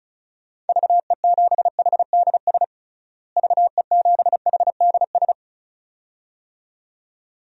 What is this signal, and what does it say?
Morse code, heard as a keyed tone, about 36 wpm, 700 Hz. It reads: VE7HDS VE7HDS
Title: DXCC-100 Call Signs - 6 Characters - Repeated 2x - 35wpm